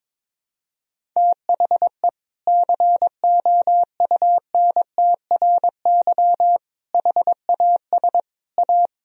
Morse code, heard as a keyed tone, about 22 wpm, 700 Hz. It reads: THE COUNTRY HAS A
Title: Sentences from Top 1000 Words - Review Part 1 - 22wpm